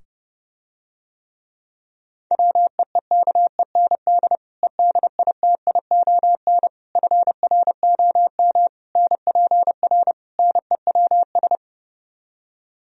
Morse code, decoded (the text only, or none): WEEKEND EDITION FROM NPR NEWS